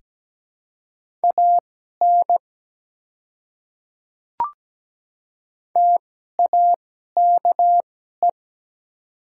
Morse code, heard as a keyed tone, about 17 wpm, spaced as for 12 wpm, 700 Hz. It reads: AN TAKE
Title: 2x Character Spacing - Top 200 Words - 17wpm